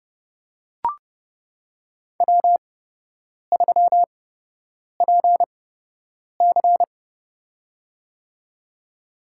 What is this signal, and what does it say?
Morse code, heard as a keyed tone, about 30 wpm, spaced as for 8 wpm, 700 Hz. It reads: W3PC
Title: USA Call Signs - 4 Characters - 8x Character Spacing - 30wpm